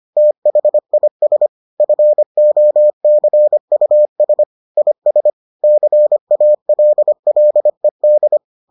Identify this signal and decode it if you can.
Morse code, heard as a keyed tone, about 25 wpm, 600 Hz. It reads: THIS FOCUS IS CALLED